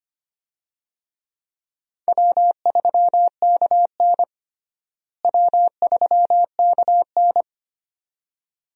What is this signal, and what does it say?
Morse code, heard as a keyed tone, about 25 wpm, 700 Hz. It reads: W3KN W3KN
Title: USA Call Signs - 4 Characters - Repeated 2x - 25wpm